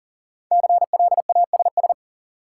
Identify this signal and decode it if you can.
Morse code, heard as a keyed tone, about 40 wpm, 700 Hz. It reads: CLASS
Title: Sentences from Top 1400 Words - New Word Drill Alt - 40wpm